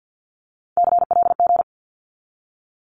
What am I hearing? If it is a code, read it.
Morse code, heard as a keyed tone, about 50 wpm, 700 Hz. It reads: CLG
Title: Abbreviations 50wpm